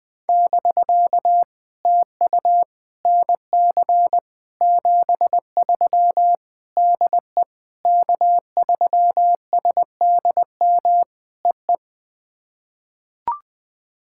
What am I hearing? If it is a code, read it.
Morse code, heard as a keyed tone, about 21 wpm, 700 Hz. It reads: <BK> TU NC 73 DE K3SDM EE
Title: POTA QSO Practice - 20wpm